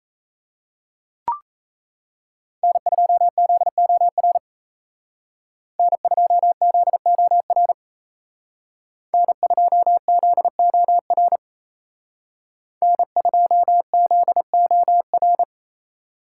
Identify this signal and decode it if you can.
Morse code, heard as a keyed tone, about 36 wpm, 700 Hz. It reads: N2ZOR N2ZOR N2ZOR N2ZOR